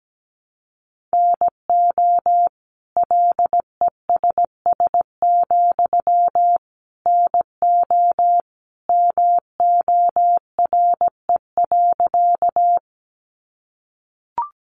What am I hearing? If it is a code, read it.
Morse code, heard as a keyed tone, about 17 wpm, 700 Hz. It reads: NO LESS, NO MORE.
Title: Sentences from Top 400 Words 17wpm